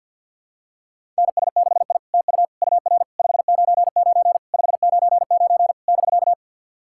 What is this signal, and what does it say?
Morse code, heard as a keyed tone, about 45 wpm, 690 Hz. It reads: NS6I TU UR 599 599 <BK>
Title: SOTA QSO Practice 50wpm